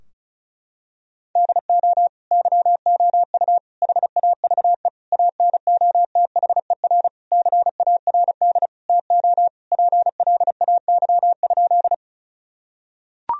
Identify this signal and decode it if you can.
Morse code, heard as a keyed tone, about 34 wpm, 700 Hz. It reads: DO YOU HAVE ANOTHER CARD TO PLAY?
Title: Sentences from Top 1000 Words - Review Part 4 - 35wpm